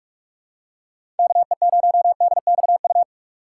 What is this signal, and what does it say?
Morse code, heard as a keyed tone, about 41 wpm, 700 Hz. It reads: KE0DXU
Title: USA Call Signs - 6 Characters - 2x Character Spacing - 45wpm